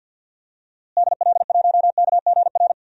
Morse code, heard as a keyed tone, about 52 wpm, 700 Hz. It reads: DL1KGR
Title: DXCC-100 Call Signs - 6 Characters - Repeated 3x - 50wpm